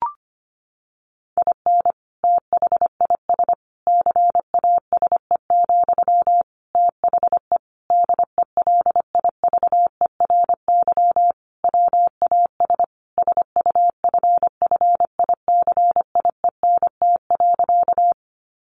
Morse code, heard as a keyed tone, about 25 wpm, 700 Hz. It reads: IN THIS CASE, THE DELIVERY WAS SUFFICIENT.
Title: Sentences from Top 1500 Words 25wpm